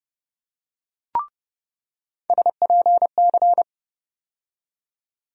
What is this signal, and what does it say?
Morse code, heard as a keyed tone, about 31 wpm, 700 Hz. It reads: SPC
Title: USA Call Sign Postfixes - 3 Letters 30wpm